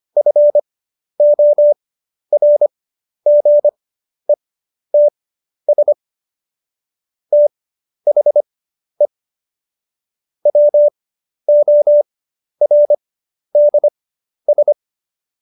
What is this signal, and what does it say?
Morse code, heard as a keyed tone, about 25 wpm, spaced as for 11 wpm, 590 Hz. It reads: FORGETS THE WORDS
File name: Jokes-25-char-10-eff